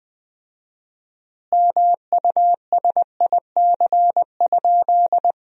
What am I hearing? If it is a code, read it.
Morse code, heard as a keyed tone, about 20 wpm, 700 Hz. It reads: MUSIC?
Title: 4x Word-Spacing - Sentences from Top 300 Words - 20wpm